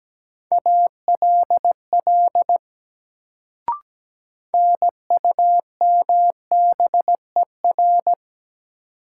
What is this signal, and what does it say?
Morse code, heard as a keyed tone, about 17 wpm, 700 Hz. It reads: ALL NUMBER
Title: Sets of 2 Words - New Word Drill - 17wpm